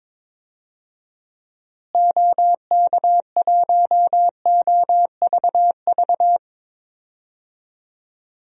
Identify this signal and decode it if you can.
Morse code, heard as a keyed tone, about 22 wpm, 700 Hz. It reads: OK1OVV
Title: DXCC-100 Call Signs - 6 Characters - Repeated 2x - 22wpm